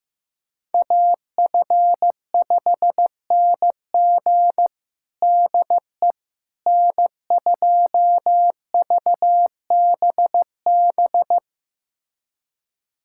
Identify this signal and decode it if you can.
Morse code, heard as a keyed tone, about 15 wpm, 700 Hz. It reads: AF5NG DE N2VBB
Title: USA Callsign DE Callsign - 5 Characters - 15wpm